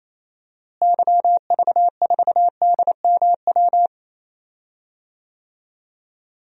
Morse code, heard as a keyed tone, about 27 wpm, 700 Hz. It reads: YV4DMW
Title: DXCC-100 Call Signs - 6 Characters - Repeated 2x - 28wpm